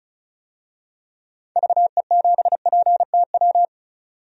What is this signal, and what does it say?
Morse code, heard as a keyed tone, about 36 wpm, 700 Hz. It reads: VE7PTW
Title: DXCC-100 Call Signs - 6 Characters - 35wpm